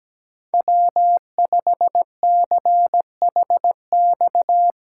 Morse code, heard as a keyed tone, about 17 wpm, 700 Hz. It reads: W5CHX
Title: USA Callsign DE Callsign 17wpm